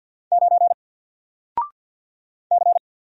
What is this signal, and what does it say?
Morse code, heard as a keyed tone, about 50 wpm, 700 Hz. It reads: 9 C
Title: Single Character (Letter:Number:Punctuation:Prosign) Alt 50wpm